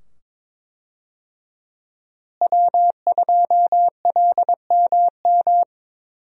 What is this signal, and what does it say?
Morse code, heard as a keyed tone, about 22 wpm, 700 Hz. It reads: W2LMM
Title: USA Call Signs - 5 Characters - Repeated 2x - 22wpm